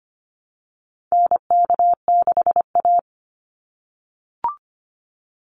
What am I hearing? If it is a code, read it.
Morse code, heard as a keyed tone, about 25 wpm, 700 Hz. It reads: NK6A
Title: NA SOTA Chaser Callsigns 25wpm